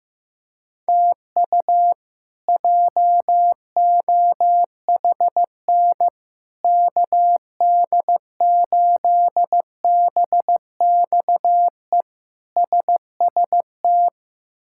Morse code, heard as a keyed tone, about 15 wpm, 700 Hz. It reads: TU JOHN KD8BXE SST
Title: K1USN SST QSO Practice - 15wpm